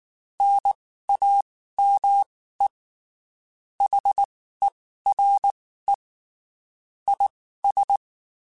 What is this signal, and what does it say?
Morse code, heard as a keyed tone, about 19 wpm, spaced as for 14 wpm, 800 Hz. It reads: NAME HERE IS
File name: (spaced) / QSOs Spaced-mp3s / kc7haa @18 sp